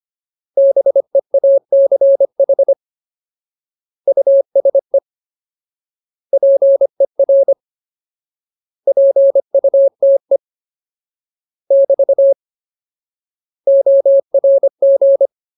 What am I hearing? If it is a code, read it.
Morse code, heard as a keyed tone, about 25 wpm, 550 Hz. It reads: BEACH USE PER PUTE <BT> ORG
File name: Ultimate Word List / 25WPM Ultimate